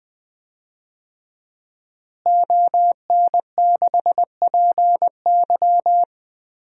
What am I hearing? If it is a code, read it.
Morse code, heard as a keyed tone, about 20 wpm, 700 Hz. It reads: ON6PY
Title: DXCC-100 Call Signs - 5 Characters - Repeated 2x - 20wpm